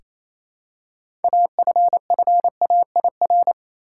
Morse code, heard as a keyed tone, about 28 wpm, 700 Hz. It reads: AFFAIR
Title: Six Letter Words - 28wpm